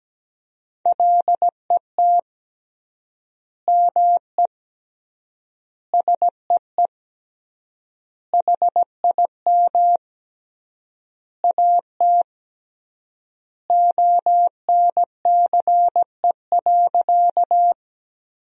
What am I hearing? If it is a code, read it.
Morse code, heard as a keyed tone, about 17 wpm, 700 Hz. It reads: LET ME SEE HIM AT ONCE.
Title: 3x Word-Spacing - Sentences from Top 300 Words - 17wpm